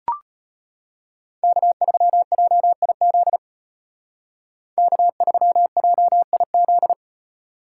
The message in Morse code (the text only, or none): K3JIZ K3JIZ